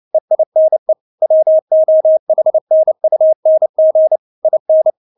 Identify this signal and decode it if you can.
Morse code, heard as a keyed tone, about 29 wpm, 630 Hz. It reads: EINE WOHNUNG IN
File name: Witz10 29wpm